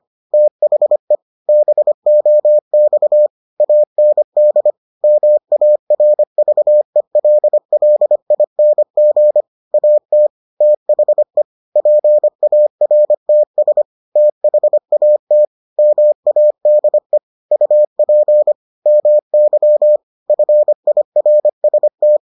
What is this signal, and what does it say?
Morse code, heard as a keyed tone, about 25 wpm, 600 Hz. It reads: THE BOX AND MARVELLING AT THE PARTS THAT MADE UP MY FIRST